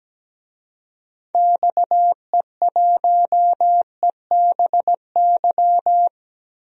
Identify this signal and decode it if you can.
Morse code, heard as a keyed tone, about 17 wpm, 700 Hz. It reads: XE1EBY
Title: DXCC-100 Call Signs - 6 Characters - 17wpm